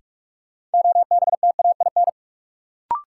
Morse code, heard as a keyed tone, about 44 wpm, 700 Hz. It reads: OBTAIN
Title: Sentences from Top 1300 Words - New Word Drill Alt - 45wpm